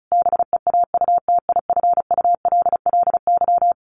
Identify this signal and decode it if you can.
Morse code, heard as a keyed tone, about 34 wpm, 700 Hz. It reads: BEAUTIFULLY
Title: A Sound Of Thunder 35wpm